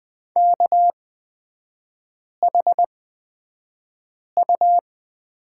Morse code, heard as a keyed tone, about 20 wpm, spaced as for 5 wpm, 690 Hz. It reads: KHU
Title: 229 Lesson 37e - Letter x - Three Characters 20wpm